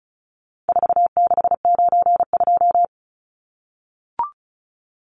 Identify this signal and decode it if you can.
Morse code, heard as a keyed tone, about 34 wpm, 700 Hz. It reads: 4692
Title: Numbers - 4-Digit 35wpm